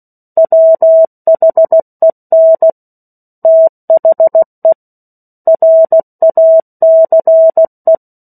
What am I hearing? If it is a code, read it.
Morse code, heard as a keyed tone, about 16 wpm, 650 Hz. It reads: WHEN THE RACE
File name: 03-Chapter-03 16 WPM